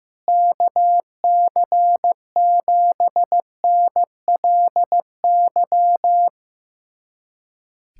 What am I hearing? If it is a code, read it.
Morse code, heard as a keyed tone, about 15 wpm, 700 Hz. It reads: KC7NLY
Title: USA Callsign DE Callsign 15wpm